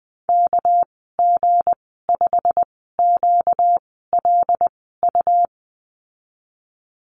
Morse code, heard as a keyed tone, about 20 wpm, spaced as for 15 wpm, 700 Hz. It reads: KG5QLU
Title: 238 Lesson 38g - Letter q - Call Signs 20wpm